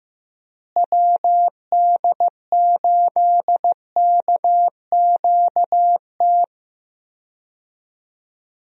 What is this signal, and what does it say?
Morse code, heard as a keyed tone, about 15 wpm, 700 Hz. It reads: WD8KQT
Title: USA Call Signs - 6 Characters - Repeated 3x - 15wpm